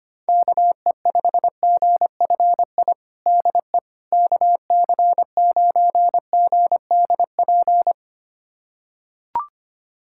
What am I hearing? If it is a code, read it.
Morse code, heard as a keyed tone, about 25 wpm, 700 Hz. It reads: KE5GFI DE KC9GDP
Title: USA Callsign DE Callsign 25wpm